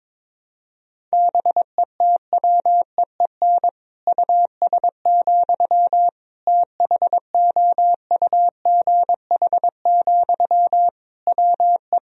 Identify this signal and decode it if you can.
Morse code, heard as a keyed tone, about 22 wpm, 700 Hz. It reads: BETWEEN US, THOUGH, WE
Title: Sentences from Top 1000 Words - Review Part 2 - 22wpm